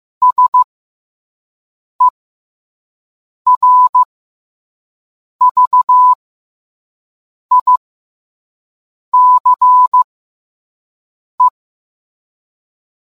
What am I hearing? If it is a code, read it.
Morse code, heard as a keyed tone, about 15 wpm, spaced as for 5 wpm, 1000 Hz. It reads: SERVICE